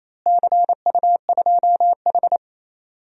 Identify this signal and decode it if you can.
Morse code, heard as a keyed tone, about 28 wpm, 700 Hz. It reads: CU2H
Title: DXCC-100 Call Signs - 4 Characters - Repeated 3x - 28wpm